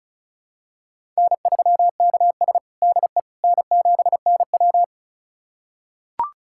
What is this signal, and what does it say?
Morse code, heard as a keyed tone, about 36 wpm, 700 Hz. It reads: N3KS DE N7NW
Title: USA Callsign DE Callsign - 4 Characters - 35wpm